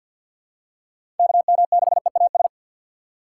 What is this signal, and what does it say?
Morse code, heard as a keyed tone, about 49 wpm, 700 Hz. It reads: KM6EAS
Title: USA Call Signs - 6 Characters - 4x Character Spacing - 50wpm